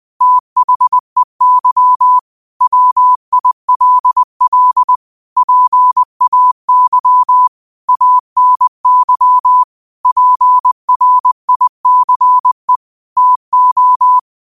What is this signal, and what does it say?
Morse code, heard as a keyed tone, about 20 wpm, 1000 Hz. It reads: THEY WILL PAY ANY PRICE TO